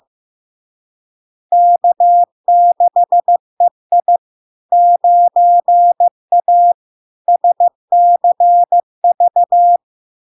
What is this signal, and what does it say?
Morse code, heard as a keyed tone, about 15 wpm, 700 Hz. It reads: K6EI 9A SCV
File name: LARK-FD-2022-Log-in-CW-15wpm-700-Hz